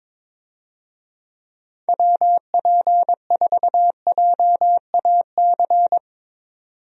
Morse code, heard as a keyed tone, about 22 wpm, 700 Hz. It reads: WP4JAC
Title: USA Call Signs - 6 Characters - 22wpm